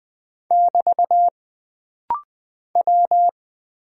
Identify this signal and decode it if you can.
Morse code, heard as a keyed tone, about 20 wpm, 700 Hz. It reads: <BT> W
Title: Single Character (Letter:Number:Punctuation:Prosign) - Rapid-Fire - 20wpm